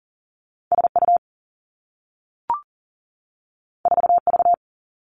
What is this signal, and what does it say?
Morse code, heard as a keyed tone, about 41 wpm, 700 Hz. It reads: S U 4 V
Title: Two Challenging Letter-Number Combinations 40wpm